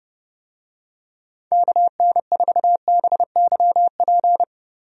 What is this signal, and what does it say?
Morse code, heard as a keyed tone, about 31 wpm, 700 Hz. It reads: KN4BYP
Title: USA Call Signs - 6 Characters - Repeated 2x - 30wpm